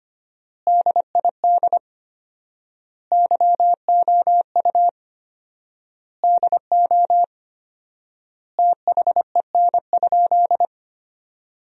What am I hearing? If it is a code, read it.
Morse code, heard as a keyed tone, about 25 wpm, 700 Hz. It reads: DID YOU DO THEN?
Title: 4x Word-Spacing - Sentences from Top 100 Words - 25wpm